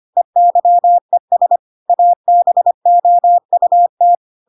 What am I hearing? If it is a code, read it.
Morse code, heard as a keyed tone, about 25 wpm, 700 Hz. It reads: EYES ABOUT
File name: Pinocchio / Pinocchio ch1 25 25 ik1hge